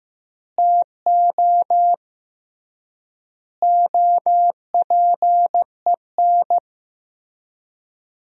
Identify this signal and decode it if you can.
Morse code, heard as a keyed tone, about 15 wpm, 700 Hz. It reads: TO OPEN
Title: 3x Word-Spacing - Sentences from Top 300 Words - 15wpm